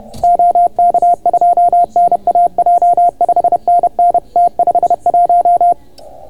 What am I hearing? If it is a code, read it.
Morse code, heard as a keyed tone, about 29 wpm, 690 Hz. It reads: OK2NAJ 5NN T51